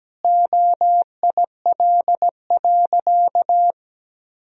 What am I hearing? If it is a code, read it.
Morse code, heard as a keyed tone, about 17 wpm, 700 Hz. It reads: OIL.
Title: Sentences from Top 1000 Words - Review Part 4 - 17wpm